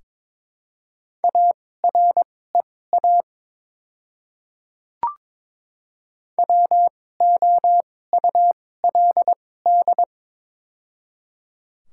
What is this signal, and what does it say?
Morse code, heard as a keyed tone, about 22 wpm, spaced as for 16 wpm, 700 Hz. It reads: AREA WOULD
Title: 2x Character Spacing - Top 500 Words - 22wpm